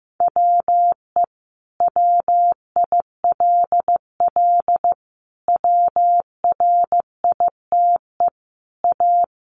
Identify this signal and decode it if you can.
Morse code, heard as a keyed tone, about 15 wpm, 700 Hz. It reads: WE WILL WRITE A
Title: Sentences from Top 1300 Words 15wpm